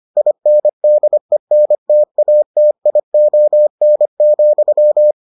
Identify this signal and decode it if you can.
Morse code, heard as a keyed tone, about 25 wpm, 600 Hz. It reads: INDENTATION,